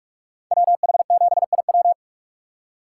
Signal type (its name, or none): Morse code